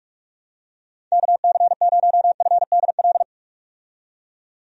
Morse code, heard as a keyed tone, about 47 wpm, 700 Hz. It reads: KC0FDL
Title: USA Call Signs - 6 Characters - 45wpm